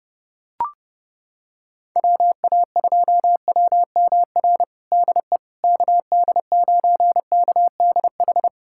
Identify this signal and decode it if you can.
Morse code, heard as a keyed tone, about 30 wpm, 700 Hz. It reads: WA2WMR DE KD9KDH
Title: USA Callsign DE Callsign - 6 Characters - 30wpm